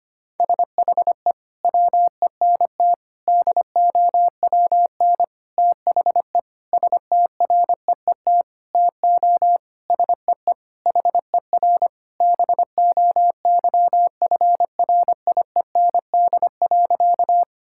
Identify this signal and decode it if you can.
Morse code, heard as a keyed tone, about 25 wpm, 700 Hz. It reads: SHE WENT DOWN THE STREET TO SEE HER BOYFRIEND.
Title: Sentences from Top 1000 Words - Review Part 5 - 25wpm